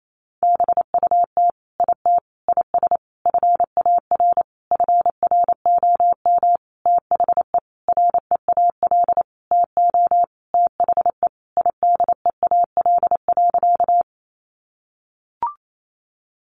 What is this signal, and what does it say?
Morse code, heard as a keyed tone, about 28 wpm, 700 Hz. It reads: BUT IT IS FAR FROM THE REAL TO THE IDEAL.
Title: Sentences from Top 1400 Words 28wpm